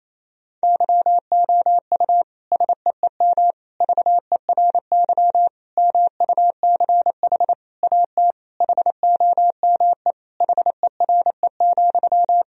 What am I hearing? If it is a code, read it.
Morse code, heard as a keyed tone, about 28 wpm, 700 Hz. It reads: YOU SEEM VERY MUCH AT HOME HERE,
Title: Sentences from Top 1000 Words - Review Part 1 - 28wpm